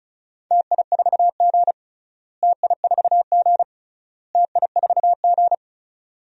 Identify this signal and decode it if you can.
Morse code, heard as a keyed tone, about 34 wpm, 700 Hz. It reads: TI4G TI4G TI4G
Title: DXCC-100 Call Signs - 4 Characters - Repeated 3x - 35wpm